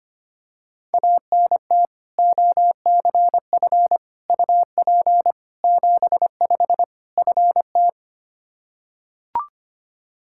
Morse code, heard as a keyed tone, about 25 wpm, 700 Hz. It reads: ANT OCF UP 75 FT
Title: Practice QSO - Exchange 2 - Info - 25wpm